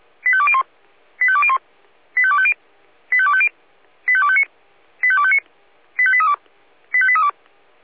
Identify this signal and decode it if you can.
Five-tone selective call (ZVEI), heard as quick runs of five tones, tones of about 70 ms. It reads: 85201 85201 85250 85250 85259 85259 85832 85832